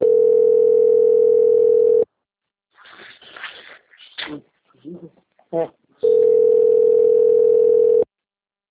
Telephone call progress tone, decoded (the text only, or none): ringback tone